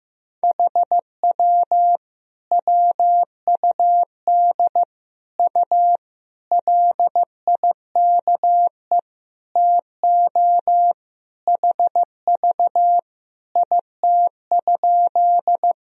Morse code, heard as a keyed tone, about 15 wpm, 700 Hz. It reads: HW WUD U LIKE TO HV IT?
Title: Sentences from Top 100 Words and Abbreviations 15wpm